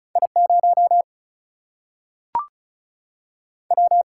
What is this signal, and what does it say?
Morse code, heard as a keyed tone, about 36 wpm, 700 Hz. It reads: I 0 W
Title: Single-Letter Single-Number 35wpm